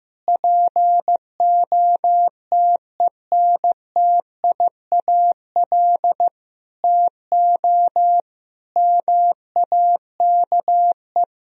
Morse code, heard as a keyed tone, about 16 wpm, 700 Hz. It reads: POTENTIAL TO MAKE
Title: Sentences from Top 1000 Words - Review Part 4 - 15wpm